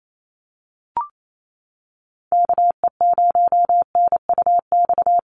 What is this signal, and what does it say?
Morse code, heard as a keyed tone, about 28 wpm, 700 Hz. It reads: KE0NUX
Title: USA Call Signs Repeated 3x 28wpm